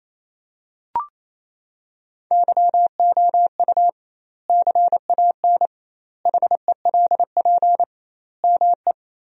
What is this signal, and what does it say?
Morse code, heard as a keyed tone, about 28 wpm, 700 Hz. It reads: YOU CAN HELP ME
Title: 2x Word-Spacing - Sentences from Top 200 Words - 28wpm